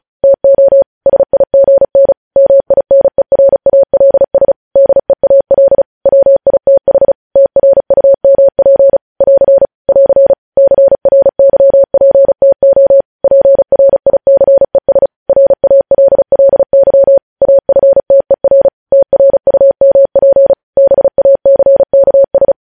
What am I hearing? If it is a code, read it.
Morse code, heard as a keyed tone, about 36 wpm, 550 Hz. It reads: TO SIGN MINERALS DEAL WITH TRUMP <AR> <AR> CRYPTO PRICES RALLY AFTER TRUMP BACKS
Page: News Headlines in Morse Code at 35 WPM